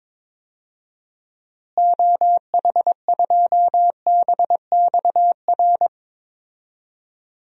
Morse code, heard as a keyed tone, about 22 wpm, 700 Hz. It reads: OH2BXR
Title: DXCC-100 Call Signs - 6 Characters - Repeated 2x - 22wpm